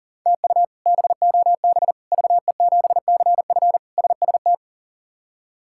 Morse code, heard as a keyed tone, about 40 wpm, 700 Hz. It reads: TU BOB VE7CF SST
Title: K1USN SST QSO Practice - 40wpm